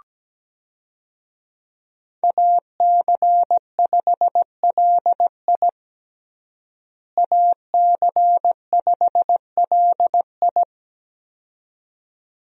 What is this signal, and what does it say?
Morse code, heard as a keyed tone, about 17 wpm, 700 Hz. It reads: AC5LI AC5LI
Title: USA Call Signs - 5 Characters - Repeated 2x - 17wpm